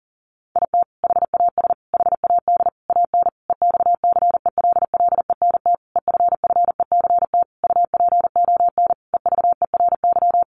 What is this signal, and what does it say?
Morse code, heard as a keyed tone, about 40 wpm, 700 Hz. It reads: IT HAS HAD AN EXCELLENT EFFECT UPON EVERY
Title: Sentences from Top 1100 Words 40wpm